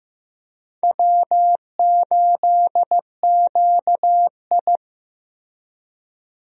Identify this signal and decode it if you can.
Morse code, heard as a keyed tone, about 15 wpm, 700 Hz. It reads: W8QI
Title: USA Call Signs - 4 Characters - Repeated 3x - 15wpm